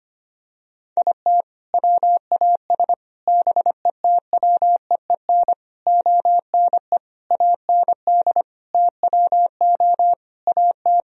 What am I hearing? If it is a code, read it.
Morse code, heard as a keyed tone, about 25 wpm, 700 Hz. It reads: IT WAS BETWEEN ONE AND TWO AT
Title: Sentences from Top 1000 Words - Review Part 1 - 25wpm